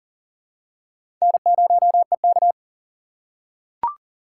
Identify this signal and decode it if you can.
Morse code, heard as a keyed tone, about 39 wpm, 700 Hz. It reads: N0EK
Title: USA Call Signs - 4 Characters - 6x Character Spacing - 40wpm